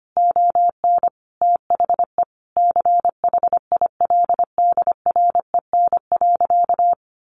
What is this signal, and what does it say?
Morse code, heard as a keyed tone, about 25 wpm, 700 Hz. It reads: ON THE CHILDREN.
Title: Sentences from Top 300 Words 25wpm